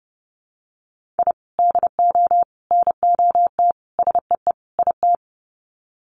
Morse code, heard as a keyed tone, about 30 wpm, 700 Hz. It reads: I DO NOT SEE IT
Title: Sets of 5 Words Encore2 30wpm